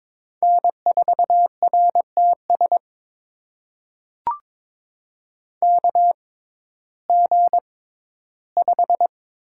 Morse code, heard as a keyed tone, about 23 wpm, spaced as for 14 wpm, 700 Hz. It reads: N4RTS KG5
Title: USA Call Signs - 5 Characters - 6x Character Spacing - 22wpm